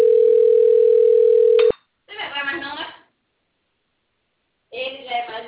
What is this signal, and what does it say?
Ringback tone (North America), 440 + 480 Hz, with speech.
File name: rg-1600-unknown- - - .57615